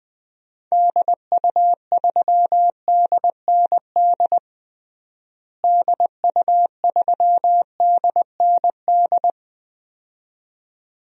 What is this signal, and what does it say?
Morse code, heard as a keyed tone, about 20 wpm, 700 Hz. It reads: DU3DND DU3DND
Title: DXCC-100 Call Signs - 6 Characters - Repeated 2x - 20wpm